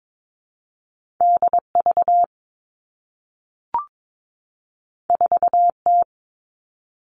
Morse code, heard as a keyed tone, about 22 wpm, 700 Hz. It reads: D V 4 T
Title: Two Challenging Letter-Number Combinations 22wpm